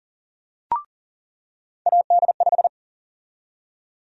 Morse code, heard as a keyed tone, about 40 wpm, 700 Hz. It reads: AD5
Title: USA Call Sign Prefixes - 40wpm